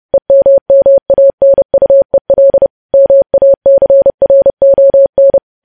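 Morse code, heard as a keyed tone, about 30 wpm, 550 Hz. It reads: EMMANUEL MACRON
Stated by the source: Morse code news headlines sent at 30, preceded by vvv vvv.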